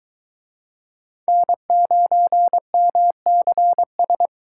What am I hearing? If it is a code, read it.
Morse code, heard as a keyed tone, about 23 wpm, 700 Hz. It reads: N9MCS